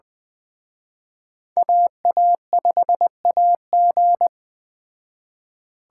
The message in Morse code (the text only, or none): AA5AG